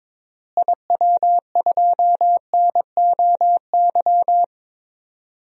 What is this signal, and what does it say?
Morse code, heard as a keyed tone, about 22 wpm, 700 Hz. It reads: IW2NOY
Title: DXCC-100 Call Signs - 6 Characters - Repeated 2x - 22wpm